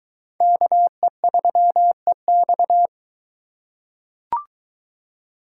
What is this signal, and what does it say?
Morse code, heard as a keyed tone, about 21 wpm, 700 Hz. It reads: KE3EX